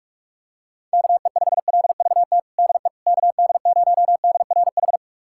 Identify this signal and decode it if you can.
Morse code, heard as a keyed tone, about 42 wpm, 700 Hz. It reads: KE5LVT DE KD0DRH
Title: USA Callsign DE Callsign - 6 Characters - 45wpm